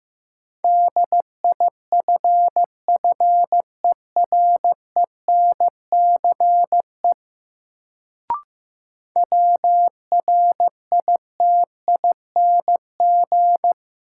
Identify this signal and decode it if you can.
Morse code, heard as a keyed tone, about 15 wpm, 700 Hz. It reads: DIFFERENCE WRITING
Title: Sentences from Top 800 Words - New Word Drill - 15wpm